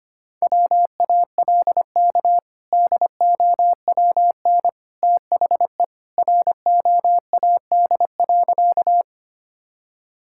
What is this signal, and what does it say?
Morse code, heard as a keyed tone, about 25 wpm, 700 Hz. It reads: WALK DOWN THE ROAD.
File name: practice / 25wpm / Sentences with 2-Word Phrases - 25wpm